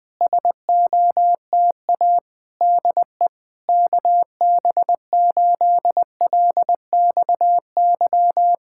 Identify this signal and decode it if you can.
Morse code, heard as a keyed tone, about 20 wpm, 700 Hz. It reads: SOTA DE KB8LXY
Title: Speed-Racing Common QSO Phrases - 20wpm